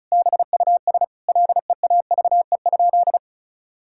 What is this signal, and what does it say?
Morse code, heard as a keyed tone, about 34 wpm, 700 Hz. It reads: BUS LEAVE?
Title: Sentences from Top 1000 Words - Review Part 6 - 35wpm